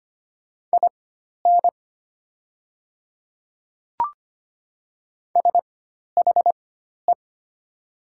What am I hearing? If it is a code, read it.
Morse code, heard as a keyed tone, about 25 wpm, spaced as for 12 wpm, 700 Hz. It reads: IN SHE
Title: 4x Character Spacing - Top 100 Words - 25wpm